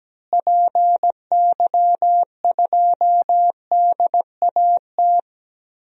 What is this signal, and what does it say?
Morse code, heard as a keyed tone, about 17 wpm, 700 Hz. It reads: PY2DAT
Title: DXCC-100 Call Signs - 6 Characters - Repeated 3x - 17wpm